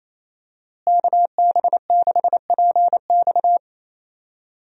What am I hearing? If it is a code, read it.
Morse code, heard as a keyed tone, about 27 wpm, 700 Hz. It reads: KB6PX
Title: USA Call Signs - 5 Characters - Repeated 3x - 28wpm